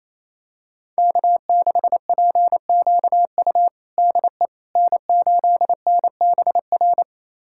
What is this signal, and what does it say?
Morse code, heard as a keyed tone, about 28 wpm, 700 Hz. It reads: K6PQU DE N8NBR
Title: USA Callsign DE Callsign - 5 Characters - 28wpm